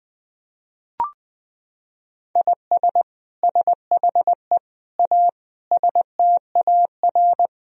Morse code, heard as a keyed tone, about 20 wpm, 700 Hz. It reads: IS SHE A STAR
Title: Sentences from Top 1000 Words - Review Part 2 - 20wpm